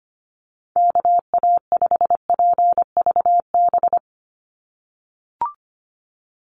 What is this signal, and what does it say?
Morse code, heard as a keyed tone, about 25 wpm, 700 Hz. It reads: KA5PVB
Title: NA SOTA Chaser Callsigns 25wpm